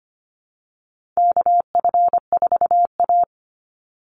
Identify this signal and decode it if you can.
Morse code, heard as a keyed tone, about 25 wpm, 700 Hz. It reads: KF4A
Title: Call Signs Repeated 2x - 25wpm